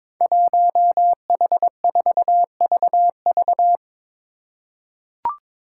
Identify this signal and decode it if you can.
Morse code, heard as a keyed tone, about 22 wpm, 700 Hz. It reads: 1H4VV
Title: Five Challenging Letter-Number Combinations 22wpm